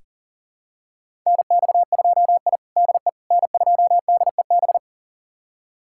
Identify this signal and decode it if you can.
Morse code, heard as a keyed tone, about 40 wpm, 700 Hz. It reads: NX2I DE N2DEB
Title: USA Callsign DE Callsign 40wpm